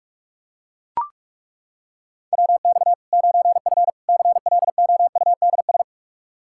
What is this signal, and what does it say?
Morse code, heard as a keyed tone, about 41 wpm, 700 Hz. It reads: WX 9F CLOUDS
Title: K1USN SST QSO Practice 45wpm